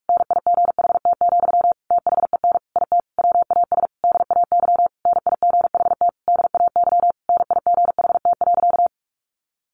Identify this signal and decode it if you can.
Morse code, heard as a keyed tone, about 45 wpm, 700 Hz. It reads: NIGHT, THEN IT WAS DAY NIGHT DAY NIGHT.
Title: A Sound Of Thunder 45wpm